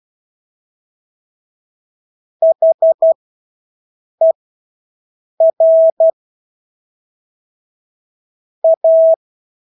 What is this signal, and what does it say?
Morse code, heard as a keyed tone, about 12 wpm, spaced as for 6 wpm, 650 Hz. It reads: HER A
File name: Engelske ord 12-6 wpm